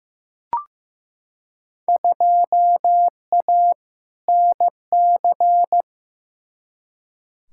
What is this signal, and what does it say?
Morse code, heard as a keyed tone, about 15 wpm, 700 Hz. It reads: 2A NC
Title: Field Day Exchanges - 15wpm